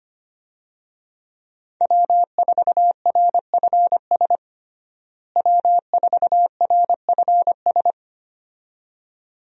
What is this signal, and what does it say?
Morse code, heard as a keyed tone, about 25 wpm, 700 Hz. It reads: W4RFS W4RFS
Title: USA Call Signs - 5 Characters - Repeated 2x - 25wpm